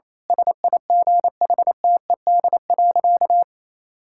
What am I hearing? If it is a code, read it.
Morse code, heard as a keyed tone, about 28 wpm, 700 Hz. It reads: SIGHTED.
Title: Pangram Challenge v2 - 28wpm